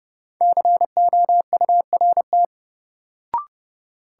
Morse code, heard as a keyed tone, about 30 wpm, 700 Hz. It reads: COURT
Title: Sentences from Top 1600 Words - New Word Drill - 30wpm